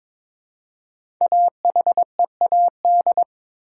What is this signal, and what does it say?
Morse code, heard as a keyed tone, about 22 wpm, 700 Hz. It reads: AHEAD
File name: 5 Letter Words - First 3 Letters Given - 22wpm